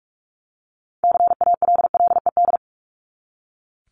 Morse code, heard as a keyed tone, about 44 wpm, 700 Hz. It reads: CALLED
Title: ED Suffix - Top 100 Words with Base Words 45wpm